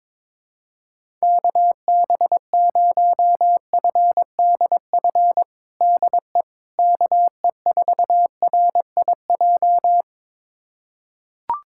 Morse code, heard as a keyed tone, about 22 wpm, 700 Hz. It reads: KB0FDF DE KE4RIJ
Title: USA Callsign DE Callsign 22wpm